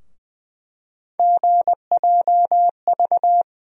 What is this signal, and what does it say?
Morse code, heard as a keyed tone, about 21 wpm, 700 Hz. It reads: GJV
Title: USA Call Sign Postfixes - 3 Letters 20wpm